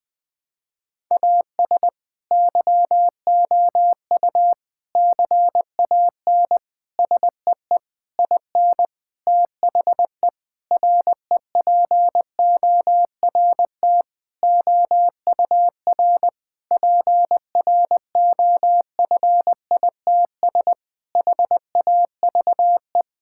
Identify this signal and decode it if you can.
Morse code, heard as a keyed tone, about 21 wpm, 700 Hz. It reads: AS YOU CAN SEE IN THE REPORT OUR PROFITS HAVE
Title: Sentences with 6-Word Phrases - Spoken Beginning - 20wpm